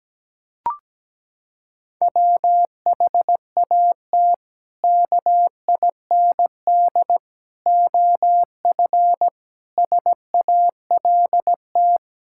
Morse code, heard as a keyed tone, about 17 wpm, 700 Hz. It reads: WHAT KIND OF SALT
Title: Sentences from Top 1000 Words - Review Part 5 - 17wpm